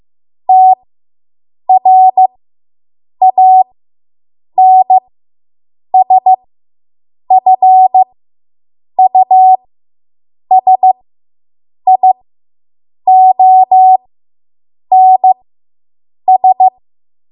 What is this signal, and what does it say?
Morse code, heard as a keyed tone, about 15 wpm, spaced as for 7 wpm, 750 Hz. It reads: TRANSFUSIONS